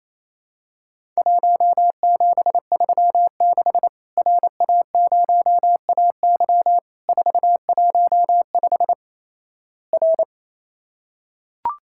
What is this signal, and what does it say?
Morse code, heard as a keyed tone, about 27 wpm, 700 Hz. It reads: 1736 RA0AY 415 R
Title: Worked All Europe DX Contest - QTC Exchange - 28wpm